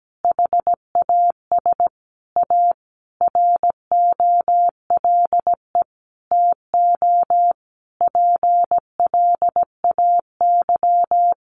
Morse code, heard as a keyed tone, about 17 wpm, 700 Hz. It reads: HAS A ROLE TO PLAY
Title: Sentences from Top 1500 Words 17wpm